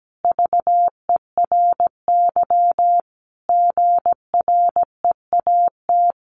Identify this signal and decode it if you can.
Morse code, heard as a keyed tone, about 17 wpm, 700 Hz. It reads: VERY GREAT
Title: Sentences from Top 1400 Words 17wpm